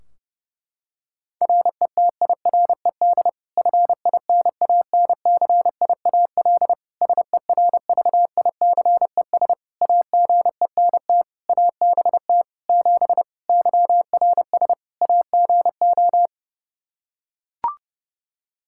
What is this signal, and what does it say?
Morse code, heard as a keyed tone, about 30 wpm, 700 Hz. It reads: RETIRED FINANCIAL SERVICES AGENT ABT 7 YRS AGO
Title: K1USN SST QSO Practice 30wpm